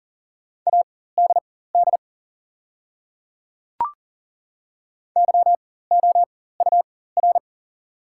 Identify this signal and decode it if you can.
Morse code, heard as a keyed tone, about 40 wpm, spaced as for 19 wpm, 700 Hz. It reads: ADD YOUR
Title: 4x Character Spacing - Top 200 Words - 40wpm